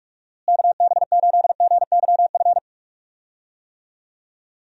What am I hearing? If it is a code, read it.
Morse code, heard as a keyed tone, about 41 wpm, 700 Hz. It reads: KB8GYF
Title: USA Call Signs - 6 Characters - Repeated 3x - 45wpm